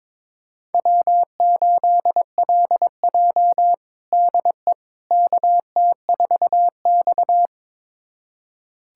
Morse code, heard as a keyed tone, about 22 wpm, 700 Hz. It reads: W8LJ DE KT4X
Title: USA Callsign DE Callsign - 4 Characters - 22wpm